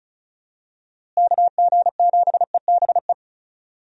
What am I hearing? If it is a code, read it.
Morse code, heard as a keyed tone, about 34 wpm, 700 Hz. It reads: KG7EBE
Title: USA Call Signs - 6 Characters - Repeated 3x - 35wpm